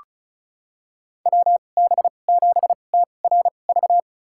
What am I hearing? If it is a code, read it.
Morse code, heard as a keyed tone, about 36 wpm, spaced as for 25 wpm, 700 Hz. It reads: WB7TRV
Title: USA Call Signs - 2x Character Spacing - 35wpm